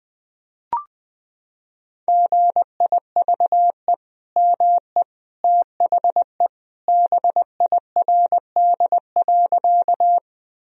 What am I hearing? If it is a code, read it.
Morse code, heard as a keyed tone, about 21 wpm, 700 Hz. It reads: GIVE ME THE BIRD.
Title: Sentences from Top 1000 Words - Review Part 2 - 20wpm